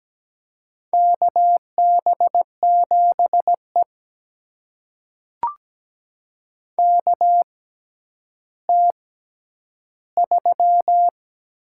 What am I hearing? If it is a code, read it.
Morse code, heard as a keyed tone, about 17 wpm, spaced as for 10 wpm, 700 Hz. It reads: KB7E KT3
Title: USA Call Signs - 6x Character Spacing - 17wpm